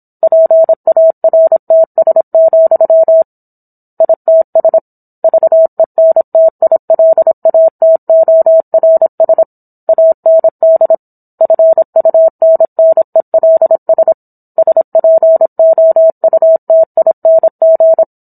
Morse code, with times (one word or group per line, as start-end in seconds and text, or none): 0.2-3.2 PARTS,
4.0-4.8 ITS
5.2-9.4 VENTILATORS
9.9-10.9 AND
11.4-14.1 FUNNELS
14.6-18.0 SPOUTING